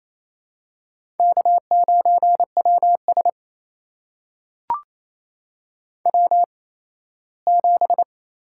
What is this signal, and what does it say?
Morse code, heard as a keyed tone, about 28 wpm, spaced as for 17 wpm, 700 Hz. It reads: K9WS W7
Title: USA Call Signs - 4 Characters - 8x Character Spacing - 28wpm